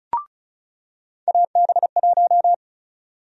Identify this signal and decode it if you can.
Morse code, heard as a keyed tone, about 36 wpm, 700 Hz. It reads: AB1
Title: USA Call Sign Prefixes - 35wpm